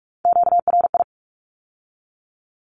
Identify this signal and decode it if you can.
Morse code, heard as a keyed tone, about 46 wpm, 700 Hz. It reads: QRI
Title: Q-Codes with Spoken Meaning - 45wpm